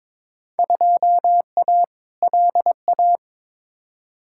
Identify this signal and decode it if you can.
Morse code, heard as a keyed tone, about 22 wpm, 700 Hz. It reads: 2A LA
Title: Field Day Exchanges - 22wpm